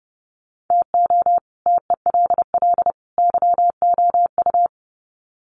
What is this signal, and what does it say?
Morse code, heard as a keyed tone, about 29 wpm, 700 Hz. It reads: TO TELL YOU
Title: Sets of 3 Words 30wpm